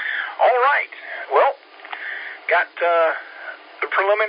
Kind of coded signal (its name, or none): Emergency Alert System (SAME) tones